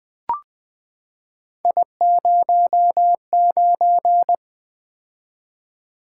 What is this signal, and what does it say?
Morse code, heard as a keyed tone, about 20 wpm, 700 Hz. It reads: I09
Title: Three Letter-Number Combinations - 20wpm